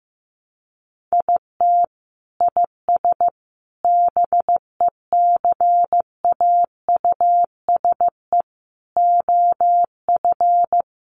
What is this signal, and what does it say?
Morse code, heard as a keyed tone, about 15 wpm, 700 Hz. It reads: IT IS BECAUSE OF
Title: Sentences from Top 600 Words 15wpm